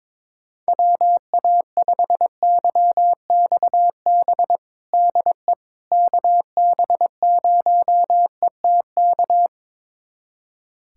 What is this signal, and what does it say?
Morse code, heard as a keyed tone, about 22 wpm, 700 Hz. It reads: WA5YXB DE KB0ETK
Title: USA Callsign DE Callsign - 6 Characters - 22wpm